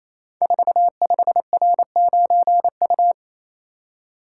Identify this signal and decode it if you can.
Morse code, heard as a keyed tone, about 28 wpm, 700 Hz. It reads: 45R9U
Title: Code Groups - 5 Letter:Numbers - 28wpm